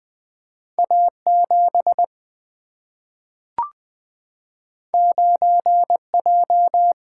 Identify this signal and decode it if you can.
Morse code, heard as a keyed tone, about 20 wpm, 700 Hz. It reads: A7 9J
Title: Single-Letter Single-Number 20wpm